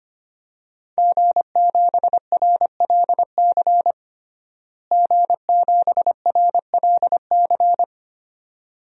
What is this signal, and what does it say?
Morse code, heard as a keyed tone, about 25 wpm, 700 Hz. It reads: G7RLC G7RLC
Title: DXCC-100 Call Signs - 5 Characters - Repeated 2x - 25wpm